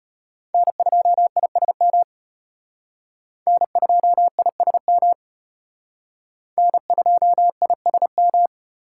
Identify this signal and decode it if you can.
Morse code, heard as a keyed tone, about 33 wpm, 700 Hz. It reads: N2ISM N2ISM N2ISM